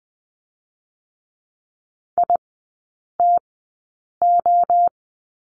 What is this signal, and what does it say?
Morse code, heard as a keyed tone, about 20 wpm, spaced as for 8 wpm, 700 Hz. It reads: ITO
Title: 074 Lesson 13e - Letter L - Three Characters 20wpm